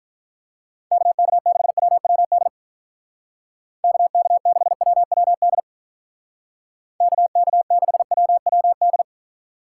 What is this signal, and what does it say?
Morse code, heard as a keyed tone, about 45 wpm, 700 Hz. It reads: KK6WWD KK6WWD KK6WWD